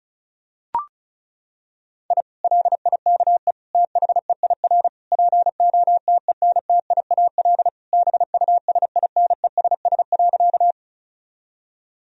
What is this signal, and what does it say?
Morse code, heard as a keyed tone, about 34 wpm, 700 Hz. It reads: I LIKE THEIR POTENTIAL BUSINESS.
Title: Sentences from Top 1000 Words - Review Part 4 - 35wpm